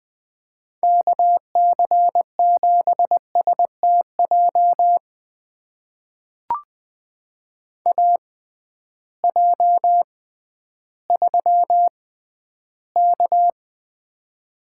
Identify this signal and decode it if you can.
Morse code, heard as a keyed tone, about 20 wpm, spaced as for 12 wpm, 700 Hz. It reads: KC7STJ AJ3K
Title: USA Call Signs - 6x Character Spacing - 20wpm